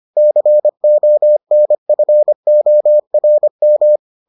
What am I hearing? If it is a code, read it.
Morse code, heard as a keyed tone, about 25 wpm, 600 Hz. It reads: CONFORM